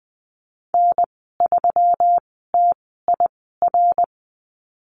Morse code, heard as a keyed tone, about 20 wpm, spaced as for 15 wpm, 700 Hz. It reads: N3TIR
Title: 154 Lesson 25g - Letter p - Call Signs 20wpm